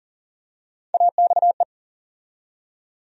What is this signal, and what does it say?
Morse code, heard as a keyed tone, about 40 wpm, 700 Hz. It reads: AXE
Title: USA Call Sign Postfixes - 3 Letters 40wpm